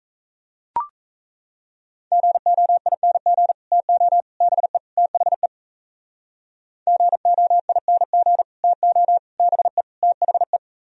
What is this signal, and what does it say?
Morse code, heard as a keyed tone, about 36 wpm, 700 Hz. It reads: GOING TO BE THE GOING TO BE THE